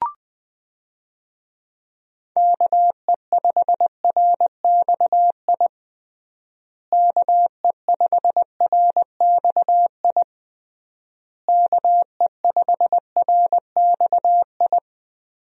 Morse code, heard as a keyed tone, about 20 wpm, 700 Hz. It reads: KE5RXI KE5RXI KE5RXI
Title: USA Call Signs - 6 Characters - Repeated 3x - 20wpm